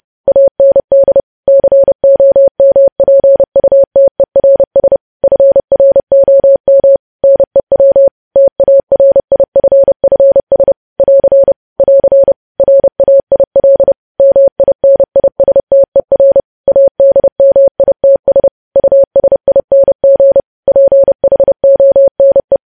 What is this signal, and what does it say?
Morse code, heard as a keyed tone, about 30 wpm, 550 Hz. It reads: AND COMPUTERS FROM NEW TARIFFS <AR> <AR> RAIL MINISTER ADMITS USING PHONE
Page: news headlines in morse code at 30 wpm